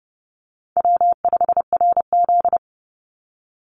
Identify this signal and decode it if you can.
Morse code, heard as a keyed tone, about 30 wpm, 700 Hz. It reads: W5RZ
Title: Easy Call Signs - 30wpm